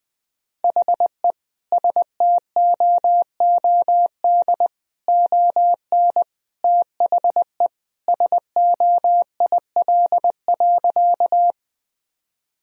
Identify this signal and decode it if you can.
Morse code, heard as a keyed tone, about 20 wpm, 700 Hz. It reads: HE STOOD ON THE SOIL.
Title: Sentences from Top 1000 Words - Review Part 4 - 20wpm